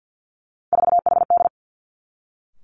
Morse code, heard as a keyed tone, about 50 wpm, 700 Hz. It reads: 4HD
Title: Three Challenging Letter-Number Combinations 50wpm